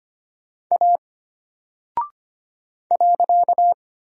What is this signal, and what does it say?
Morse code, heard as a keyed tone, about 25 wpm, 700 Hz. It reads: A .
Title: Single Character (Letter:Number:Punctuation:Prosign) Alt 25wpm